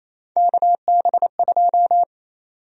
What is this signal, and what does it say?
Morse code, heard as a keyed tone, about 28 wpm, 700 Hz. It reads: KB2
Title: USA Call Sign Prefixes - Repeated 2x - 28wpm